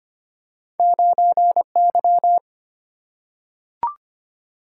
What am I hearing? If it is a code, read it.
Morse code, heard as a keyed tone, about 25 wpm, 700 Hz. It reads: 9Y
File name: Two Letter-Number Combinations - 25wpm